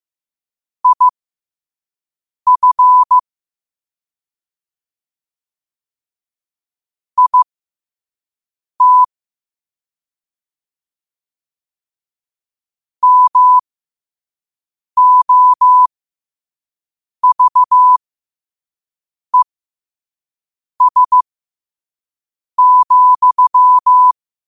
Morse code, heard as a keyed tone, about 15 wpm, spaced as for 5 wpm, 1000 Hz. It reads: IF IT MOVES,
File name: qotdfiles / 5wpm